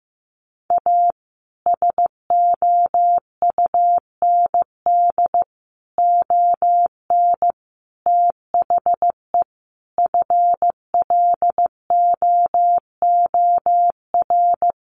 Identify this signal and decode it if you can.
Morse code, heard as a keyed tone, about 15 wpm, 700 Hz. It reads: A SOUND ON THE FLOOR
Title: A Sound Of Thunder 15wpm